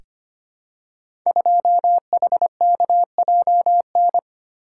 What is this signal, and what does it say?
Morse code, heard as a keyed tone, about 25 wpm, 700 Hz. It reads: 2HKJN
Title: Code Groups - 5 Letter:Numbers - 25wpm